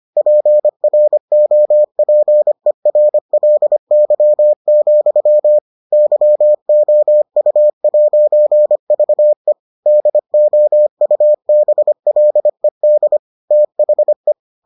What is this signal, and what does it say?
Morse code, heard as a keyed tone, about 25 wpm, 600 Hz. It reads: PROPERLY, YOU'VE DOUBLED THE